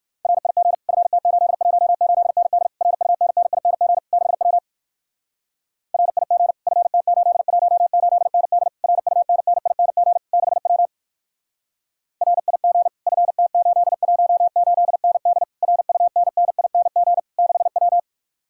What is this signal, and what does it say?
Morse code, heard as a keyed tone, about 59 wpm, 700 Hz. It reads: RIG FT818ND RUNNING 6W RIG FT818ND RUNNING 6W RIG FT818ND RUNNING 6W